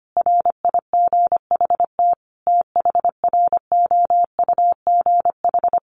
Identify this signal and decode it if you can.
Morse code, heard as a keyed tone, about 25 wpm, 700 Hz. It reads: RIGHT THROUGH
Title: Sentences from Top 200 Words 25wpm